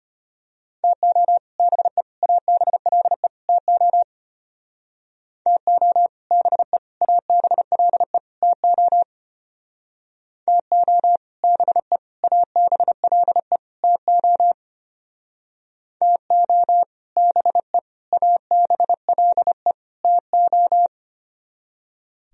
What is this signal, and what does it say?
Morse code, heard as a keyed tone, about 30 wpm, 700 Hz. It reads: TO BE ABLE TO TO BE ABLE TO TO BE ABLE TO TO BE ABLE TO
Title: Speed-Racing - Sets of 4 Words 25wpm